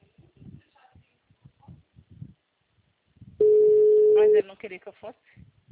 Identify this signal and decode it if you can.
Ringback tone (Europe, 425 Hz), with speech.
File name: out- -2016- - - .84928